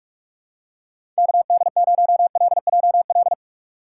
Morse code, heard as a keyed tone, about 46 wpm, 700 Hz. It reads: KD0LJL
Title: USA Call Signs - 6 Characters - 2x Character Spacing - 45wpm